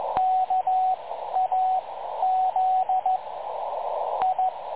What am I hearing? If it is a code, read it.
Morse code, heard as a keyed tone, about 15 wpm, spaced as for 9 wpm, 750 Hz, with radio noise. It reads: KAZI
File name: dlazaawansowanych / nagraniazpasma / websdr recording 3565.0kHz